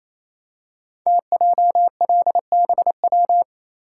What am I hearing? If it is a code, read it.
Morse code, heard as a keyed tone, about 27 wpm, 700 Hz. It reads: TJLBW
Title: Code Groups - 5 Letter:Numbers - 28wpm